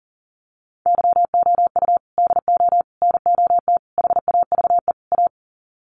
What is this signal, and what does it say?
Morse code, heard as a keyed tone, about 40 wpm, 700 Hz. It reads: YOU DO NOT HAVE A
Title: Sets of 5 Words 40wpm